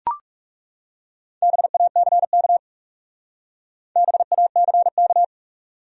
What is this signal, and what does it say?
Morse code, heard as a keyed tone, about 35 wpm, 700 Hz. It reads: BACK BACK
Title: Speed-Racing Top 300 Words 30wpm